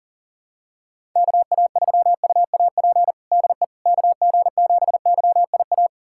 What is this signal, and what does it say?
Morse code, heard as a keyed tone, about 40 wpm, 700 Hz. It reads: KA3UAP DE KG7YIA
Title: USA Callsign DE Callsign - 6 Characters - 40wpm